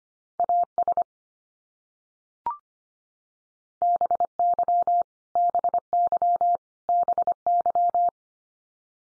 Morse code, heard as a keyed tone, about 25 wpm, 700 Hz. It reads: AS BY BY BY
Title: Top 100 Words 25wpm x3